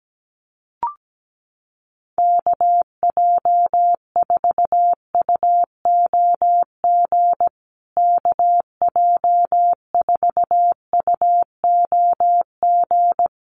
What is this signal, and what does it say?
Morse code, heard as a keyed tone, about 17 wpm, 700 Hz. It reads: KJ4UOG KJ4UOG
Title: USA Call Signs Repeated 3x 17wpm